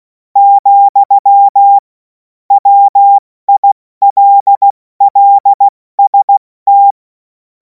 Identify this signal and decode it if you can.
Morse code, heard as a keyed tone, about 16 wpm, 800 Hz. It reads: , WILLST
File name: Morsekurs / Witze / Witz 21. April 2026 70 F80